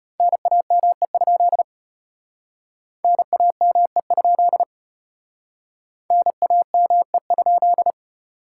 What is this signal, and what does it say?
Morse code, heard as a keyed tone, about 32 wpm, 700 Hz. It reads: NAME? NAME? NAME?